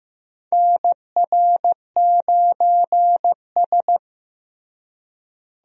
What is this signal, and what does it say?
Morse code, heard as a keyed tone, about 15 wpm, 700 Hz. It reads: NR9S
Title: USA Call Signs - 4 Characters - Repeated 2x - 15wpm